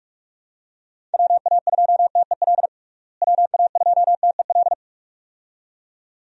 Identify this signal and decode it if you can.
Morse code, heard as a keyed tone, about 47 wpm, 700 Hz. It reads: WA2TEL WA2TEL
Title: USA Call Signs - 6 Characters - Repeated 2x - 45wpm